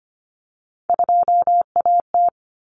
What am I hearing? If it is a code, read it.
Morse code, heard as a keyed tone, about 25 wpm, 700 Hz. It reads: 2AT
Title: Common Cut Numbers - 3-Digit 25wpm